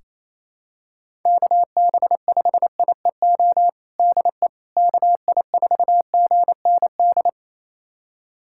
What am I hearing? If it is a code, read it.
Morse code, heard as a keyed tone, about 28 wpm, 700 Hz. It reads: KB5IEO DE KI4GND
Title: USA Callsign DE Callsign 28wpm